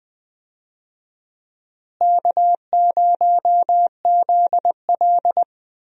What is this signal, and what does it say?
Morse code, heard as a keyed tone, about 20 wpm, 700 Hz. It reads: K0ZL
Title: USA Call Signs - 4 Characters - Repeated 2x - 20wpm